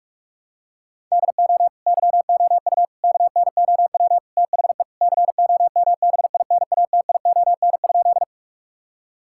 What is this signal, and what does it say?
Morse code, heard as a keyed tone, about 46 wpm, 700 Hz. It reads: DO YOU KNOW THE COMBINATION?
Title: Sentences from Top 1000 Words - Review Part 6 - 45wpm